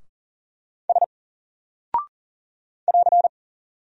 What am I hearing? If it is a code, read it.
Morse code, heard as a keyed tone, about 40 wpm, 700 Hz. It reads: S <AR>
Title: Single Character (Letter:Number:Punctuation:Prosign) Alt 40wpm